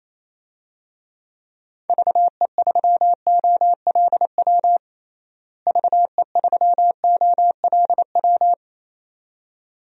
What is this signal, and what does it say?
Morse code, heard as a keyed tone, about 28 wpm, 700 Hz. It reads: VE3OLW VE3OLW
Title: DXCC-100 Call Signs - 6 Characters - Repeated 2x - 28wpm